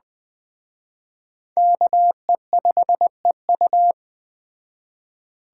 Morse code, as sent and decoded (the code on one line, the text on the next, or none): -.- . ..... ..-
KE5EU